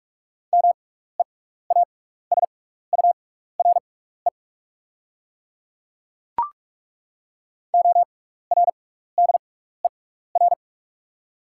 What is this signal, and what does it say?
Morse code, heard as a keyed tone, about 42 wpm, spaced as for 15 wpm, 690 Hz. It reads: MEASURE ORDER
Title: 6x Character Spacing - Top 500 Words - 45wpm